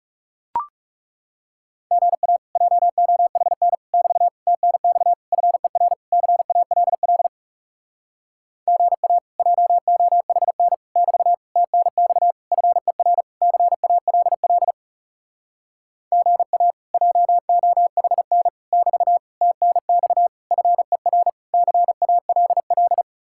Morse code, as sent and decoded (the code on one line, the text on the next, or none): --. .- / .--- --- .... -. / -...- / - -. -..- / ..-. .-. / -.-. .- .-.. .-.. / --. .- / .--- --- .... -. / -...- / - -. -..- / ..-. .-. / -.-. .- .-.. .-.. / --. .- / .--- --- .... -. / -...- / - -. -..- / ..-. .-. / -.-. .- .-.. .-..
GA JOHN <BT> TNX FER CALL GA JOHN <BT> TNX FER CALL GA JOHN <BT> TNX FER CALL